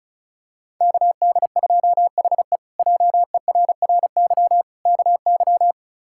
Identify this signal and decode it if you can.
Morse code, heard as a keyed tone, about 36 wpm, 700 Hz. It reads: KD2HE JERRY KY
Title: K1USN SST QSO Practice - 35wpm